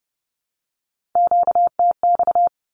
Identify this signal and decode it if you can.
Morse code, heard as a keyed tone, about 30 wpm, 700 Hz. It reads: QTX
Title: Q-Codes 30wpm